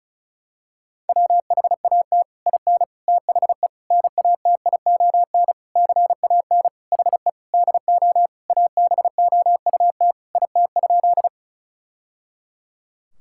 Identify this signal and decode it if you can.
Morse code, heard as a keyed tone, about 36 wpm, 700 Hz. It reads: WHAT IN THE NATION CAN HE DO ABOUT IT?
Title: Sentences from Top 1000 Words - Review Part 6 - 35wpm